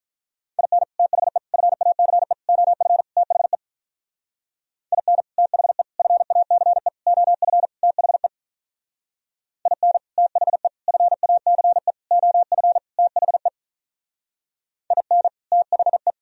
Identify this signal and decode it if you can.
Morse code, heard as a keyed tone, about 43 wpm, 700 Hz. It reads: IN THE FACE OF THE IN THE FACE OF THE IN THE FACE OF THE IN THE
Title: Speed-Racing - Sets of 5 Words 35wpm